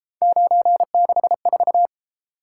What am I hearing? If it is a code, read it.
Morse code, heard as a keyed tone, about 33 wpm, 700 Hz. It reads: 964